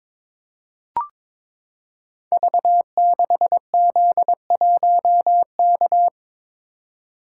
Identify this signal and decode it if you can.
Morse code, heard as a keyed tone, about 22 wpm, 700 Hz. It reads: V6Z1K
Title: Code Groups - 5 Letter:Numbers - 22wpm